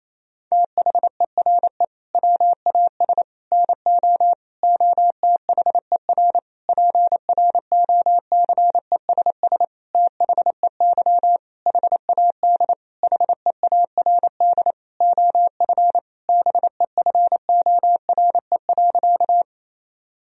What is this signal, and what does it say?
Morse code, heard as a keyed tone, about 27 wpm, 700 Hz. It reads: THERE WAS NO OTHER PROCESS THEY HAD HEARD OF BEFORE.
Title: Sentences from Top 1000 Words - Review Part 3 - 28wpm